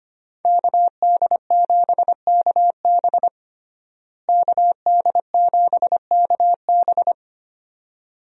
Morse code, heard as a keyed tone, about 25 wpm, 700 Hz. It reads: KD7KB KD7KB
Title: USA Call Signs - 5 Characters - Repeated 2x - 25wpm